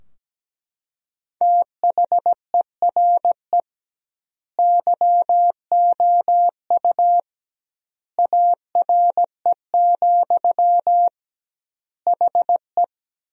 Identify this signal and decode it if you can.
Morse code, heard as a keyed tone, about 17 wpm, 700 Hz. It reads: THERE YOU ARE, HE
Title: 2x Word-Spacing - Sentences from Top 100 Words - 17wpm